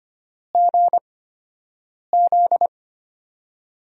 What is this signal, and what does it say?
Morse code, heard as a keyed tone, about 25 wpm, spaced as for 7 wpm, 700 Hz. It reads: GZ
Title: USA Call Signs - 6 Characters - 8x Character Spacing - 25wpm